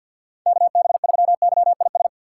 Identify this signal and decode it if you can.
Morse code, heard as a keyed tone, about 46 wpm, 700 Hz. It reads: KB3YIS
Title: USA Call Signs - 6 Characters - 6x Character Spacing - 50wpm